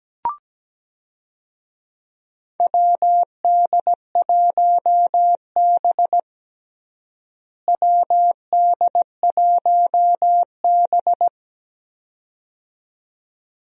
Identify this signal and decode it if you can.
Morse code, heard as a keyed tone, about 17 wpm, 700 Hz. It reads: WD1B WD1B
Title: USA Call Signs - 4 Characters - Repeated 2x - 17wpm